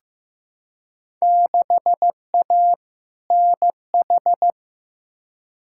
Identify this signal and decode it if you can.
Morse code, heard as a keyed tone, about 15 wpm, 700 Hz. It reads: 6A NH
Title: Field Day Exchanges - 15wpm